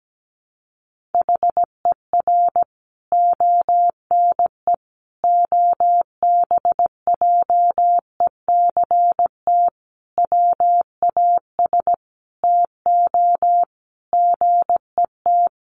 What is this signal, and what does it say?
Morse code, heard as a keyed tone, about 17 wpm, 700 Hz. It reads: HER ONE OBJECT WAS TO GET
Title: Sentences from Top 1500 Words 17wpm